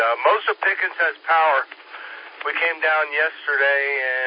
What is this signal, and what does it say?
Emergency Alert System (SAME) tones, heard as harsh buzzing bursts, with speech. It is heard from 0.8 to 3.7 s.